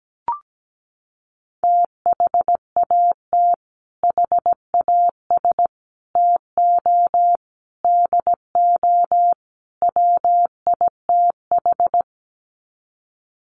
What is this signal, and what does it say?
Morse code, heard as a keyed tone, about 17 wpm, 700 Hz. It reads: THAT HAS TO DO WITH
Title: Sets of 5 Words Encore2 17wpm